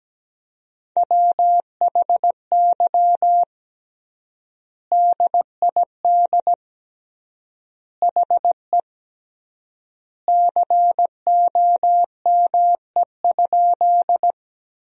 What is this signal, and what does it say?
Morse code, heard as a keyed tone, about 17 wpm, 700 Hz. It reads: WHY DID HE COME?
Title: 3x Word-Spacing - Sentences from Top 200 Words - 17wpm